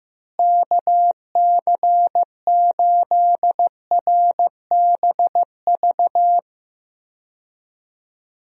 Morse code, heard as a keyed tone, about 15 wpm, 700 Hz. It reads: KC8RBV
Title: USA Call Signs - 6 Characters - Repeated 2x - 15wpm